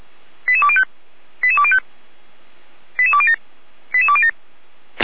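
Five-tone selective call (ZVEI), heard as quick runs of five tones, tones of about 70 ms. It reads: 88205 88205 88207 88207
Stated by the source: ZVEI five-tone call received with an SDR.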